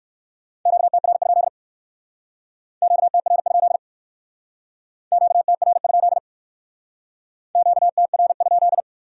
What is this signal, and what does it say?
Morse code, heard as a keyed tone, about 57 wpm, 700 Hz. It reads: QTR? QTR? QTR? QTR?